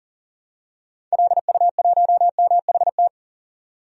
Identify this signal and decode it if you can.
Morse code, heard as a keyed tone, about 40 wpm, 700 Hz. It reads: LU1MHT
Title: DXCC-100 Call Signs - 6 Characters - 40wpm